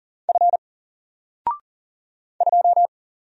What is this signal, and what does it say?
Morse code, heard as a keyed tone, about 40 wpm, 700 Hz. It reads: F 2
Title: Single Letter-Number Alt 40wpm